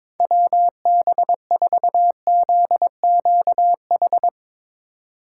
Morse code, heard as a keyed tone, about 22 wpm, 700 Hz. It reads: WB4ZQH
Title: USA Call Signs - 4x Character Spacing - 22wpm